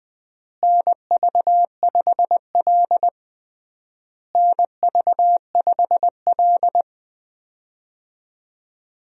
Morse code, heard as a keyed tone, about 20 wpm, 700 Hz. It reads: NV5L NV5L
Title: USA Call Signs - 4 Characters - Repeated 3x - 20wpm